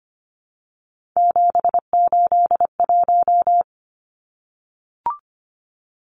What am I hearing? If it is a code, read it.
Morse code, heard as a keyed tone, about 25 wpm, 700 Hz. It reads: 781
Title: Numbers - 3-Digit 25wpm